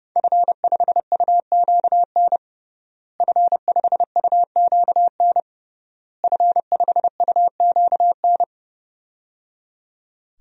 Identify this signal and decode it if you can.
Morse code, heard as a keyed tone, about 30 wpm, 700 Hz. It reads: F5UQN F5UQN F5UQN
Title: DXCC-100 Call Signs - 5 Characters - Repeated 3x - 30wpm